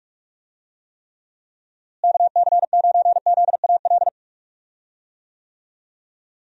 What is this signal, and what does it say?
Morse code, heard as a keyed tone, about 42 wpm, 700 Hz. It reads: KC9ZAL
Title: USA Call Signs - 6 Characters - Repeated 2x - 45wpm